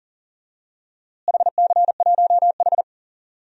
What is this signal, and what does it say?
Morse code, heard as a keyed tone, about 40 wpm, 700 Hz. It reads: HC1H
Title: DXCC-100 Call Signs - 4 Characters - Repeated 2x - 40wpm